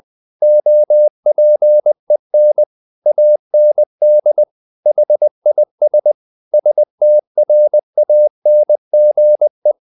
Morse code, heard as a keyed tone, about 20 wpm, 600 Hz. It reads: OPEN AND HIS STRANGE
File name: the old man and the sea part 2 20wpm